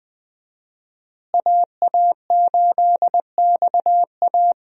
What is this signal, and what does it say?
Morse code, heard as a keyed tone, about 20 wpm, 700 Hz. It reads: AA8XA
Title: USA Call Signs - 5 Characters - Repeated 3x - 20wpm